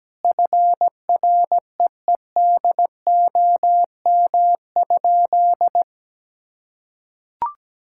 Morse code, heard as a keyed tone, about 17 wpm, 700 Hz. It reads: FREEDOM?
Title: Sentences from Top 1000 Words - Review Part 5 - 17wpm